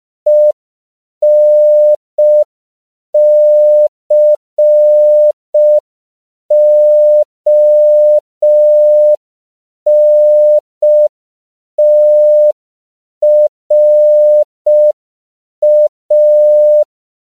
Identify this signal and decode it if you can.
Morse code, heard as a keyed tone, about 5 wpm, 600 Hz. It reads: ENCONTRA
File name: CW / 5PPM